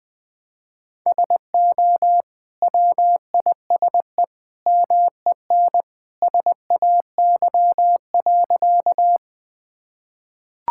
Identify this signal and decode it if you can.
Morse code, heard as a keyed tone, about 21 wpm, 700 Hz. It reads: SO WISE MEN SAY.
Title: Sentences from Top 1600 Words 20wpm